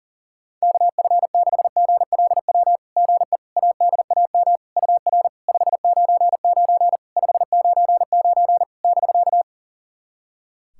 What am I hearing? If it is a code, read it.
Morse code, heard as a keyed tone, about 40 wpm, 690 Hz. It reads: KF6GLW GE ADAM UR 599 599 <BK>
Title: SOTA QSO Practice 40wpm